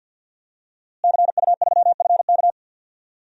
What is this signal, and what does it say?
Morse code, heard as a keyed tone, about 47 wpm, 700 Hz. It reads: CU3FK
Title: DXCC-100 Call Signs - 5 Characters - Repeated 3x - 50wpm